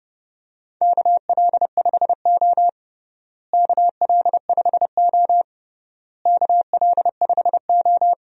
Morse code, heard as a keyed tone, about 30 wpm, 700 Hz. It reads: KL5O KL5O KL5O
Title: USA Call Signs - 4 Characters - Repeated 3x - 30wpm